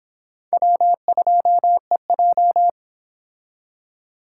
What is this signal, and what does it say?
Morse code, heard as a keyed tone, about 25 wpm, 700 Hz. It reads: W2EJ